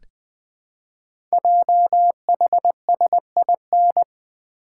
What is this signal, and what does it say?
Morse code, heard as a keyed tone, about 20 wpm, 700 Hz. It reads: JHSIN
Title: Code Groups - 5 Letters - 20wpm